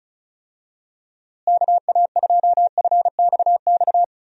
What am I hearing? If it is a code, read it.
Morse code, heard as a keyed tone, about 36 wpm, 700 Hz. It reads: KA2FXX
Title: USA Call Signs - 6 Characters - Repeated 3x - 35wpm